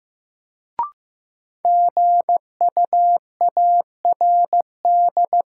Morse code, heard as a keyed tone, about 15 wpm, 700 Hz. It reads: GUARD
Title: Sentences from Top 700 Words - New Word Drill - 15wpm